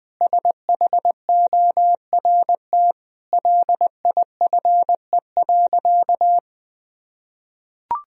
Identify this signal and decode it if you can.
Morse code, heard as a keyed tone, about 20 wpm, 700 Hz. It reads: SHORT LIFE.
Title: Sentences from Top 1000 Words - Review Part 2 - 20wpm